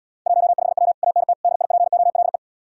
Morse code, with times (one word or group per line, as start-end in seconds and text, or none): 0.3-0.9 15W
1.0-1.3 ANT
1.4-2.4 DIPOLE